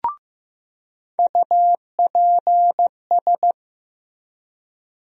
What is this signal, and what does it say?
Morse code, heard as a keyed tone, about 15 wpm, 700 Hz. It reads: UPS
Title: USA Call Sign Postfixes - 3 Letters 15wpm